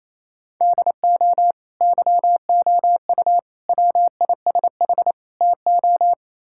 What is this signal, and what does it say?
Morse code, heard as a keyed tone, about 28 wpm, 700 Hz. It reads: DO YOU WISH TO
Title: Sentences from Top 1600 Words 28wpm